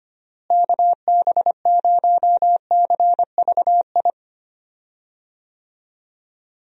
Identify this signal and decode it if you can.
Morse code, heard as a keyed tone, about 25 wpm, 700 Hz. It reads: KB0CVI
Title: USA Call Signs - 6 Characters - Repeated 2x - 25wpm